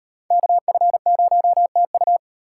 Morse code, heard as a keyed tone, about 34 wpm, 700 Hz. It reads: KF0TU